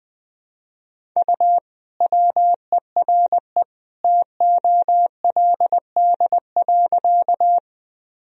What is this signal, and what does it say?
Morse code, heard as a keyed tone, about 20 wpm, 700 Hz. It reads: U WERE TOLD.
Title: Sentences from Top 400 Words and Abbreviations 20wpm